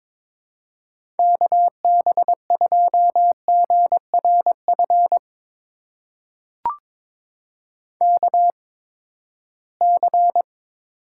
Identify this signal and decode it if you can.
Morse code, heard as a keyed tone, about 22 wpm, spaced as for 15 wpm, 700 Hz. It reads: KB2GRF KC
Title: USA Call Signs - 6 Characters - 8x Character Spacing - 22wpm